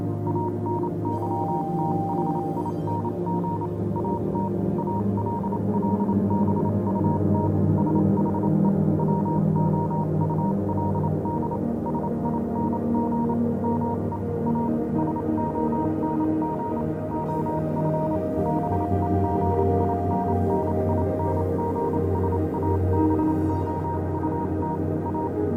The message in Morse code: ANYTHING AT ALL BUT SHE WOULD SING ME A SONG OF BEGINNINGS ABOUT A